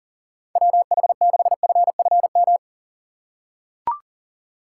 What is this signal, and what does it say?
Morse code, heard as a keyed tone, about 40 wpm, 700 Hz. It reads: WH6FFM
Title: USA Call Signs - 6 Characters - 4x Character Spacing - 40wpm